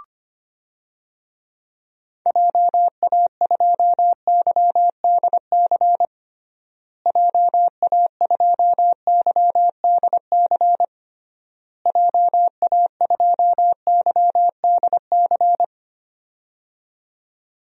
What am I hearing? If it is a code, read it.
Morse code, heard as a keyed tone, about 25 wpm, 700 Hz. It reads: JA2YDC JA2YDC JA2YDC
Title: DXCC-100 Call Signs - 6 Characters - Repeated 3x - 25wpm